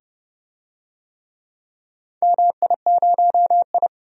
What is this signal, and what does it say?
Morse code, heard as a keyed tone, about 30 wpm, 700 Hz. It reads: MI0I
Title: DXCC-100 Call Signs - 4 Characters - 30wpm